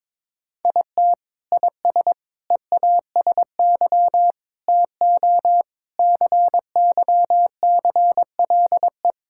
Morse code, heard as a keyed tone, about 22 wpm, 700 Hz. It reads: IT IS EASY TO CYCLE
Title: Sentences from Top 1000 Words - Review Part 6 - 22wpm